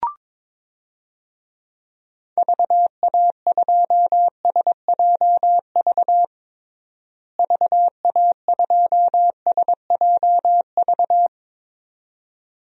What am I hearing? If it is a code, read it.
Morse code, heard as a keyed tone, about 22 wpm, 700 Hz. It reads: VA2SJV VA2SJV
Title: DXCC-100 Call Signs - 6 Characters - Repeated 2x - 22wpm